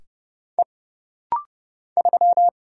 Morse code, heard as a keyed tone, about 30 wpm, 700 Hz. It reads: E 3
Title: Single Letter-Number - Rapid-Fire 30wpm